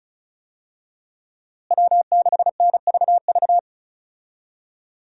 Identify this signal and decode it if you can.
Morse code, heard as a keyed tone, about 34 wpm, 700 Hz. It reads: W6NVV
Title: USA Call Signs - 5 Characters - Repeated 3x - 35wpm